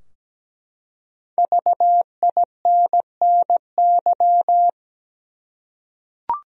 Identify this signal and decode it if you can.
Morse code, heard as a keyed tone, about 17 wpm, 700 Hz. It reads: VINNY
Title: USA Common Names - 17wpm